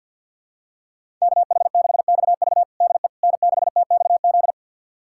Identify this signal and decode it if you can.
Morse code, heard as a keyed tone, about 51 wpm, 700 Hz. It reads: KH6XV DE N6TXZ
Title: USA Callsign DE Callsign - 5 Characters - 50wpm